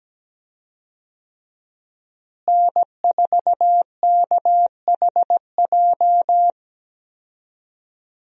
Morse code, heard as a keyed tone, about 17 wpm, 700 Hz. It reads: N4KHJ
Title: USA Call Signs - 5 Characters - Repeated 2x - 17wpm